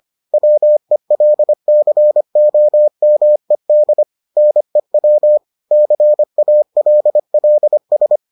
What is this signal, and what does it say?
Morse code, heard as a keyed tone, about 25 wpm, 600 Hz. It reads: WELCOMED NEW CALLS